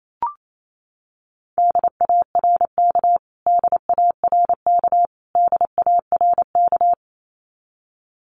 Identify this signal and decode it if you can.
Morse code, heard as a keyed tone, about 28 wpm, 700 Hz. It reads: DARK DARK DARK
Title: Top 500 Words Repeated 28wpm